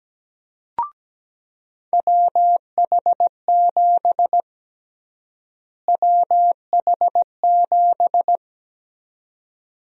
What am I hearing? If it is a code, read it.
Morse code, heard as a keyed tone, about 17 wpm, 700 Hz. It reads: WH7 WH7
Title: USA Call Sign Prefixes - Repeated 2x - 17wpm